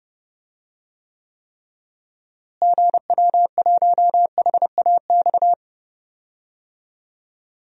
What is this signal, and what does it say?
Morse code, heard as a keyed tone, about 30 wpm, 700 Hz. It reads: GW1HAX
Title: DXCC-100 Call Signs - 6 Characters - Repeated 3x - 30wpm